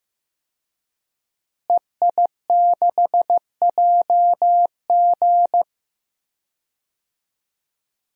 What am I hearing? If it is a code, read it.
Morse code, heard as a keyed tone, about 15 wpm, 700 Hz. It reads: EI6JG
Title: DXCC-100 Call Signs - 5 Characters - Repeated 3x - 15wpm